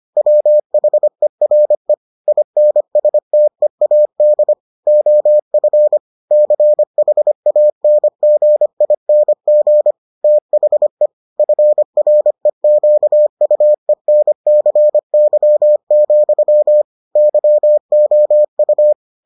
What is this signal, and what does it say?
Morse code, heard as a keyed tone, about 25 wpm, 600 Hz. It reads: WHERE INSTEAD OF CHANGING THE FREQUENCY, YOU